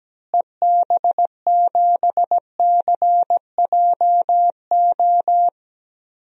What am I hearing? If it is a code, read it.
Morse code, heard as a keyed tone, about 17 wpm, 700 Hz. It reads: EB7CJO
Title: DXCC-100 Call Signs - 6 Characters - Repeated 3x - 17wpm